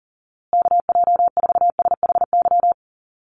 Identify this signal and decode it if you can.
Morse code, heard as a keyed tone, about 40 wpm, 700 Hz. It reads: KJ4SHY
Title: USA Call Signs Repeated 3x 40wpm